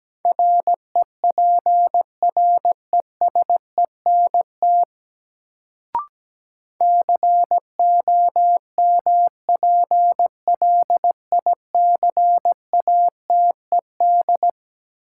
Morse code, heard as a keyed tone, about 17 wpm, 700 Hz. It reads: REPRESENT COMPLICATED
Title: Sentences from Top 1200 Words - New Word Drill Alt - 17wpm